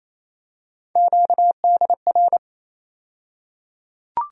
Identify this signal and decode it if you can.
Morse code, heard as a keyed tone, about 28 wpm, 700 Hz. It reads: QDR
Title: USA Call Sign Postfixes - 3 Letters 28wpm